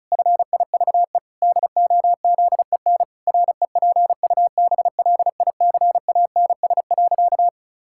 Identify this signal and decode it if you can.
Morse code, heard as a keyed tone, about 34 wpm, 700 Hz. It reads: FIVE DOZEN REPUBLICANS.
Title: Pangram Challenge v2 - 35wpm